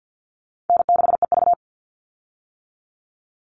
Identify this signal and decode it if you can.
Morse code, heard as a keyed tone, about 50 wpm, 700 Hz. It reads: N6EV
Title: Easy Call Signs - 50wpm